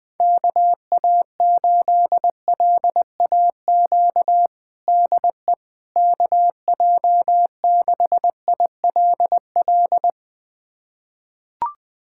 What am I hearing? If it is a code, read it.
Morse code, heard as a keyed tone, about 21 wpm, 700 Hz. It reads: KA8LAQ DE KJ6ILL
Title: USA Callsign DE Callsign 20wpm